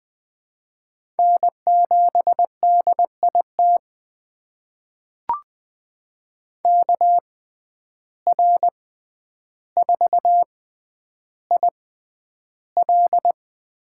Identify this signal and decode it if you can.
Morse code, heard as a keyed tone, about 20 wpm, spaced as for 10 wpm, 700 Hz. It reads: N7DIT KR4IL
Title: USA Call Signs - 5 Characters - 6x Character Spacing - 20wpm